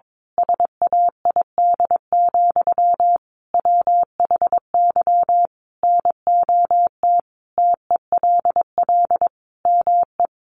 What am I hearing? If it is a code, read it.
Morse code, heard as a keyed tone, about 22 wpm, 700 Hz. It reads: SAID, WHY NOT TELL ME
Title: Sentences from Top 800 Words 22wpm